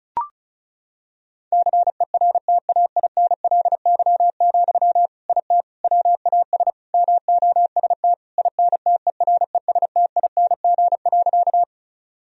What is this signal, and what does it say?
Morse code, heard as a keyed tone, about 36 wpm, 700 Hz. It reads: CERTAINLY, IT WAS MOST INTERESTING.
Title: Sentences from Top 1000 Words - Review Part 1 - 35wpm